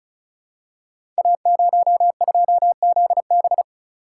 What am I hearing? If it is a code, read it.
Morse code, heard as a keyed tone, about 36 wpm, 700 Hz. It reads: A02ZB
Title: Code Groups - 5 Letter:Numbers - 35wpm